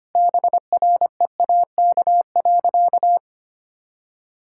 Morse code, heard as a keyed tone, about 25 wpm, 700 Hz. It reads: BREAK.
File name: practice / 25wpm / Sentences with 2-Word Phrases - 25wpm